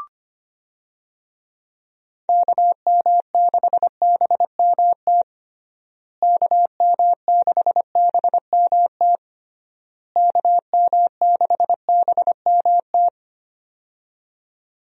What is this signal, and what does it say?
Morse code, heard as a keyed tone, about 25 wpm, 700 Hz. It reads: KM6BMT KM6BMT KM6BMT
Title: USA Call Signs - 6 Characters - Repeated 3x - 25wpm